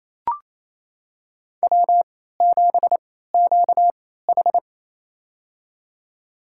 Morse code, heard as a keyed tone, about 28 wpm, spaced as for 16 wpm, 700 Hz. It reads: W7QH
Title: USA Call Signs - 4 Characters - 3x Character Spacing - 28wpm